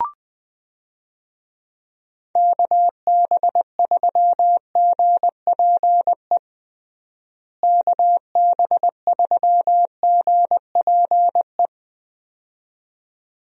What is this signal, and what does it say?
Morse code, heard as a keyed tone, about 20 wpm, 700 Hz. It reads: KB3GPE KB3GPE
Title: USA Call Signs - 6 Characters - Repeated 2x - 20wpm